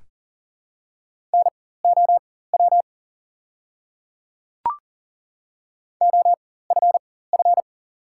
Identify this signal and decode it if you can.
Morse code, heard as a keyed tone, about 40 wpm, spaced as for 19 wpm, 700 Hz. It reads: NOW OFF
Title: 4x Character Spacing - Top 200 Words - 40wpm